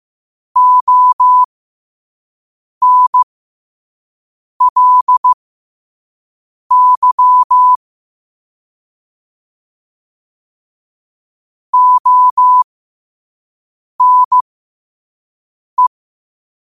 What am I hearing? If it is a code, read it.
Morse code, heard as a keyed tone, about 15 wpm, spaced as for 5 wpm, 1000 Hz. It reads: ONLY ONE
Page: Quotes for Tue, 12 Aug 2025 in Morse Code at 5 words per minute.